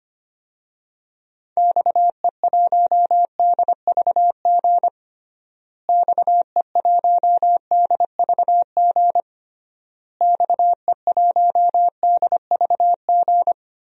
Morse code, heard as a keyed tone, about 25 wpm, 690 Hz. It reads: XE1DVG XE1DVG XE1DVG
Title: DXCC-100 Call Signs - 6 Characters - Repeated 3x - 25wpm